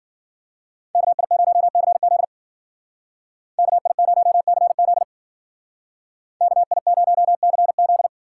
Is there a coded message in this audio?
Yes: Morse code from 0.9 to 8.1 s, heard as a keyed tone, about 51 wpm, 700 Hz. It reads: KI0CZ KI0CZ KI0CZ